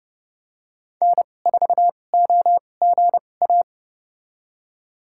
Morse code, heard as a keyed tone, about 30 wpm, spaced as for 22 wpm, 700 Hz. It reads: N4OGA
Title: USA Call Signs - 5 Characters - 2x Character Spacing - 30wpm